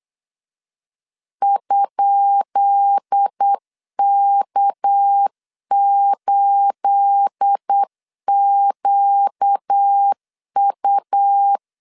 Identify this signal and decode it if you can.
Morse code, heard as a keyed tone, about 8 wpm, 800 Hz. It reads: ?K8QU
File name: LEZIONI / 10 SIGNS / 10B INT 8wpm 3534